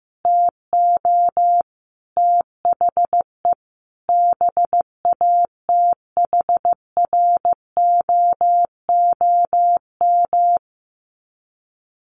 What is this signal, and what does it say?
Morse code, heard as a keyed tone, about 15 wpm, 700 Hz. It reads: TO THE BATHROOM
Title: Sets of 5 Words Encore 15wpm